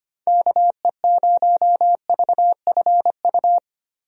Morse code, heard as a keyed tone, about 25 wpm, 700 Hz. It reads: KE0VFU
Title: USA Call Signs - 6 Characters - Repeated 2x - 25wpm